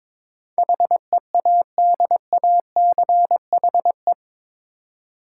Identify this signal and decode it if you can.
Morse code, heard as a keyed tone, about 22 wpm, 700 Hz. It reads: HEADACHE
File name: practice / 22wpm / Eight Letter Words - 22wpm